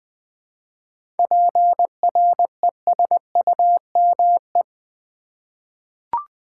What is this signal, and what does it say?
Morse code, heard as a keyed tone, about 20 wpm, 690 Hz. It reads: PRESUME
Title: 7 Letter Words - First 3 Letters Given - 20wpm